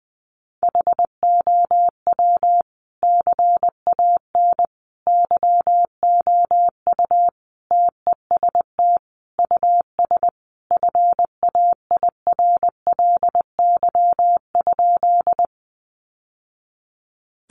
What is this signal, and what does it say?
Morse code, heard as a keyed tone, about 20 wpm, 700 Hz. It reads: HOW CAN YOU TEST US FAIRLY?
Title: Sentences from Top 1100 Words 20wpm